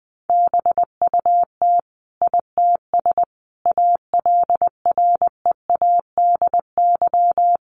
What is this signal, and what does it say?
Morse code, heard as a keyed tone, about 20 wpm, 700 Hz. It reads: BUT ITS ALREADY
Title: Sentences from Top 700 Words 20wpm